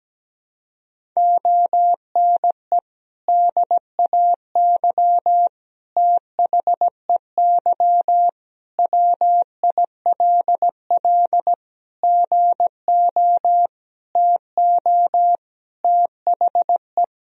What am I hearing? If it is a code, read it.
Morse code, heard as a keyed tone, about 17 wpm, 700 Hz. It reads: ONE DAY THEY WILL GO TO THE
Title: Sentences from Top 1000 Words - Review Part 1 - 17wpm